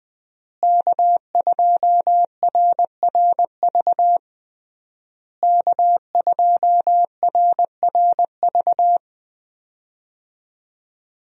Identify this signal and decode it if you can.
Morse code, heard as a keyed tone, about 20 wpm, 700 Hz. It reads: K2RRV K2RRV
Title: USA Call Signs - 5 Characters - Repeated 3x - 20wpm